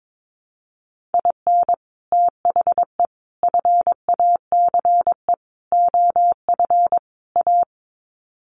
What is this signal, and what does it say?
Morse code, heard as a keyed tone, about 22 wpm, 700 Hz. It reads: IN THE FACE OF A
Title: Sets of 5 Words Encore 22wpm